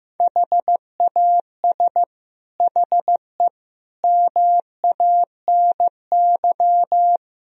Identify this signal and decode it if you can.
Morse code, heard as a keyed tone, about 15 wpm, 700 Hz. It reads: HAS HE MANY
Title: Sentences from Top 1000 Words - Review Part 1 - 15wpm